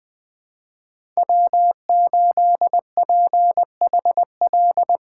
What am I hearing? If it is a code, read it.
Morse code, heard as a keyed tone, about 20 wpm, 700 Hz. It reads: W8PHL
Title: USA Call Signs - 5 Characters - 3x Character Spacing - 20wpm